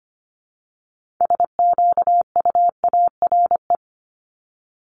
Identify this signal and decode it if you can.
Morse code, heard as a keyed tone, about 25 wpm, 700 Hz. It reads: SQUARE
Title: Multi-letter Phonemes 25wpm